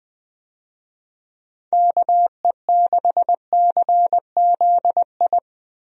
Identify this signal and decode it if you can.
Morse code, heard as a keyed tone, about 21 wpm, 700 Hz. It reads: KE6CZI
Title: USA Call Signs - 6 Characters - Repeated 2x - 20wpm